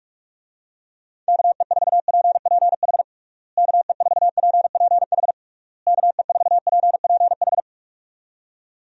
Morse code, heard as a keyed tone, about 46 wpm, 700 Hz. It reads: KE4PPH KE4PPH KE4PPH
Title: USA Call Signs - 6 Characters - Repeated 3x - 45wpm